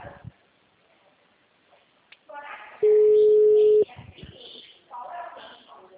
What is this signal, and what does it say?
Ringback tone (Europe, 425 Hz), with speech.
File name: out- -2023- - - .16321